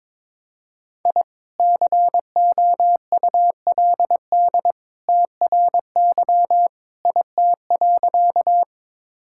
Morse code, heard as a keyed tone, about 21 wpm, 700 Hz. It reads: I COULD TRY IT.
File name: practice / 22wpm / Sentences with 2-Word Phrases - 22wpm